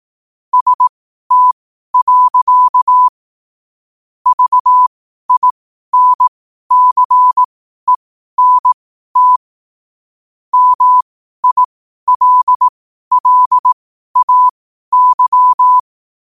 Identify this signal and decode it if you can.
Morse code, heard as a keyed tone, about 18 wpm, spaced as for 12 wpm, 1000 Hz. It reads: ST. VINCENT MILLAY